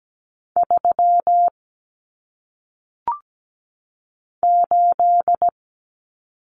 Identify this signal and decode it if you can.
Morse code, heard as a keyed tone, about 17 wpm, 700 Hz. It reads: 3 8
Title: Common Cut Numbers - 1-Digit 17wpm